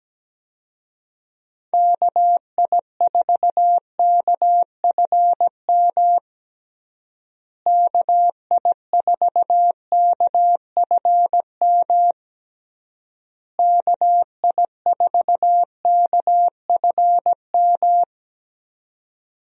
Morse code, heard as a keyed tone, about 17 wpm, 700 Hz. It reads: KI4KFM KI4KFM KI4KFM
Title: USA Call Signs - 6 Characters - Repeated 3x - 17wpm